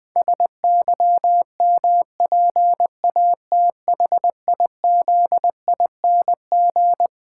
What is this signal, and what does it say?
Morse code, heard as a keyed tone, about 20 wpm, 700 Hz. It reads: SYMPATHIZING
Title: Pangram Challenge v2 - 20wpm